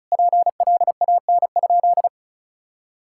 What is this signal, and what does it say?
Morse code, heard as a keyed tone, about 36 wpm, 700 Hz. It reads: PLAN?
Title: Sentences from Top 1000 Words - Review Part 6 - 35wpm